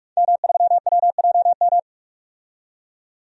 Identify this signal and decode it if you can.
Morse code, heard as a keyed tone, about 42 wpm, 700 Hz. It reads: M3WJM
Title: DXCC-100 Call Signs - 5 Characters - 45wpm